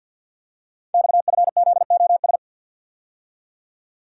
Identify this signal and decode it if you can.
Morse code, heard as a keyed tone, about 50 wpm, 700 Hz. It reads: XFZOS
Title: Code Groups - 5 Letters - 50wpm